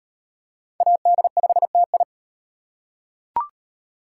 Morse code, heard as a keyed tone, about 38 wpm, 700 Hz. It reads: AD5TI